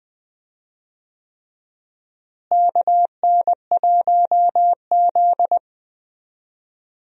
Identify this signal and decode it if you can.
Morse code, heard as a keyed tone, about 20 wpm, 700 Hz. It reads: KN1Z
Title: USA Call Signs - 4 Characters - Repeated 3x - 20wpm